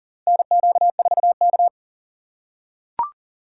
Morse code, heard as a keyed tone, about 40 wpm, 700 Hz. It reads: NQ4K
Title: USA Call Signs - 4 Characters - 6x Character Spacing - 40wpm